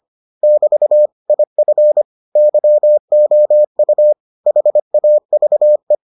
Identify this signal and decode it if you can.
Morse code, heard as a keyed tone, about 25 wpm, 600 Hz. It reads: <BT> IF YOU HAVE